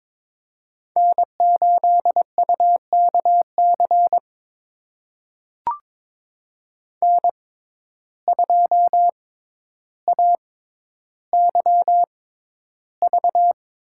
Morse code, heard as a keyed tone, about 22 wpm, spaced as for 11 wpm, 700 Hz. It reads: N8UKC N2AYV
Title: USA Call Signs - 5 Characters - 6x Character Spacing - 22wpm